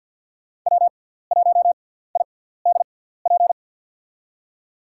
Morse code, heard as a keyed tone, about 47 wpm, 700 Hz. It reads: W 1 I D P
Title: USA Call Signs - 6x Character Spacing - 50wpm